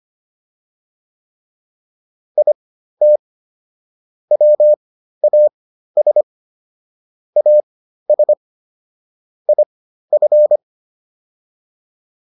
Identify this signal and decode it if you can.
Morse code, heard as a keyed tone, about 25 wpm, spaced as for 13 wpm, 600 Hz. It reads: IT WAS AS IF